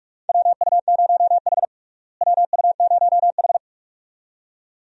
Morse code, heard as a keyed tone, about 46 wpm, 700 Hz. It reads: WU0H WU0H
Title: USA Call Signs - 4 Characters - Repeated 2x - 45wpm